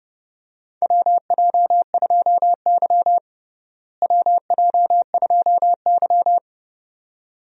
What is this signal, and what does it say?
Morse code, heard as a keyed tone, about 30 wpm, 700 Hz. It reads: WJ2Y WJ2Y
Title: USA Call Signs - 4 Characters - Repeated 2x - 30wpm